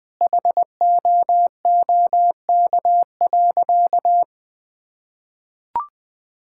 Morse code, heard as a keyed tone, about 20 wpm, 700 Hz. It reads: HOOK.
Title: Sentences from Top 1600 Words 20wpm